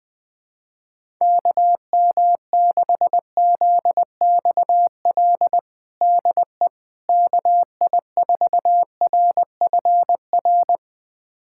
Morse code, heard as a keyed tone, about 20 wpm, 700 Hz. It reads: KM6ZXL DE KI4RFR
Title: USA Callsign DE Callsign 20wpm